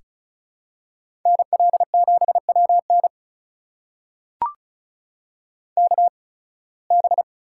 Morse code, heard as a keyed tone, about 36 wpm, spaced as for 23 wpm, 700 Hz. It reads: NL7WN KB
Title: USA Call Signs - 8x Character Spacing - 35wpm